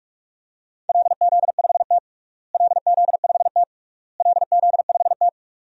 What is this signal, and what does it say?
Morse code, heard as a keyed tone, about 48 wpm, 700 Hz. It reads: LZ5T LZ5T LZ5T
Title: DXCC-100 Call Signs - 4 Characters - Repeated 3x - 45wpm